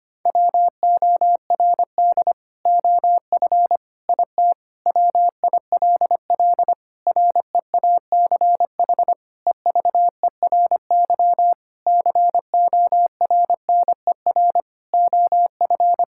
Morse code, heard as a keyed tone, about 25 wpm, 700 Hz. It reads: WORD OF IT WILL REACH EVERY CORNER OF
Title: Sentences from Top 1600 Words 25wpm